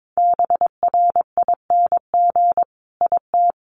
Morse code, heard as a keyed tone, about 22 wpm, 700 Hz. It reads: BRING IT
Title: Sentences from Top 1300 Words 22wpm